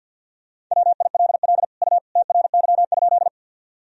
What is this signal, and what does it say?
Morse code, heard as a keyed tone, about 50 wpm, 700 Hz. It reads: WILL U TRY?
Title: Sentences from Top 200 Words and Abbreviations 50wpm